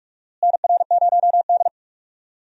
Morse code, heard as a keyed tone, about 46 wpm, 700 Hz. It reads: NR0D
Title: USA Call Signs - 4 Characters - Repeated 3x - 45wpm